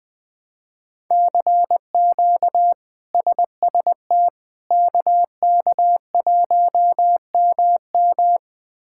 Morse code, heard as a keyed tone, about 20 wpm, 700 Hz. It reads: CQ SST KK1MM
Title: K1USN SST QSO Practice - 20wpm